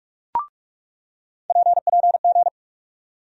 Morse code, heard as a keyed tone, about 48 wpm, 700 Hz. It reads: PPG
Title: USA Call Sign Postfixes - 3 Letters 45wpm